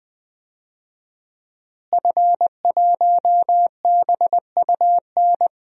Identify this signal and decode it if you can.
Morse code, heard as a keyed tone, about 21 wpm, 700 Hz. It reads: F1BUN
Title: DXCC-100 Call Signs - 5 Characters - Repeated 3x - 20wpm